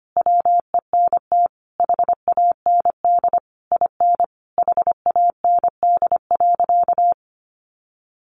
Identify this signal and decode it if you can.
Morse code, heard as a keyed tone, about 25 wpm, 700 Hz. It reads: WENT HAND IN HAND.
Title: Sentences from Top 1100 Words 25wpm